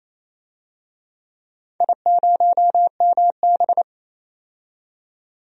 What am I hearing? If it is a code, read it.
Morse code, heard as a keyed tone, about 28 wpm, 700 Hz. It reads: I0MB
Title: DXCC-100 Call Signs - 4 Characters - 28wpm